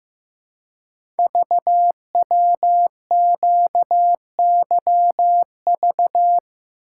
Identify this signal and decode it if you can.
Morse code, heard as a keyed tone, about 15 wpm, 700 Hz. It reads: VWQYV
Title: Code Groups - 5 Letters - 15wpm